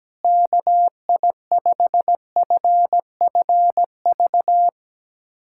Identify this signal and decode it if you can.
Morse code, heard as a keyed tone, about 17 wpm, 700 Hz. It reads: KI5FFV
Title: USA Callsign DE Callsign 17wpm